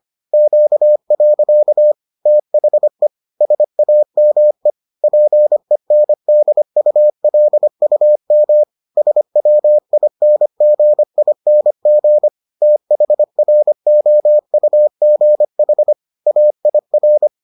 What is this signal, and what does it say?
Morse code, heard as a keyed tone, about 25 wpm, 600 Hz. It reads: Q. THE SAME PENDULUM SWINGING THROUGH AIR